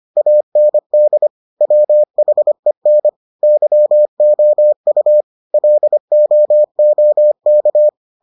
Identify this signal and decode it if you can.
Morse code, heard as a keyed tone, about 25 wpm, 600 Hz. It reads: AND WHEN YOU LOOK